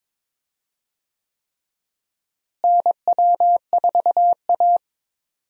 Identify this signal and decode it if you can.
Morse code, heard as a keyed tone, about 22 wpm, 700 Hz. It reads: NW4A
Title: USA Call Signs - 4 Characters - Repeated 3x - 22wpm